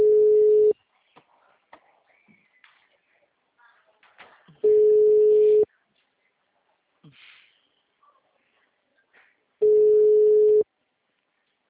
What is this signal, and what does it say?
Ringback tone (Europe, 425 Hz).